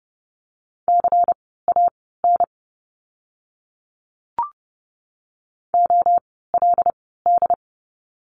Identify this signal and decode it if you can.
Morse code, heard as a keyed tone, about 30 wpm, spaced as for 17 wpm, 700 Hz. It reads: CAN OLD
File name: course / 30:10wpm / 230 Lesson 37f - Letter x - Words 30wpm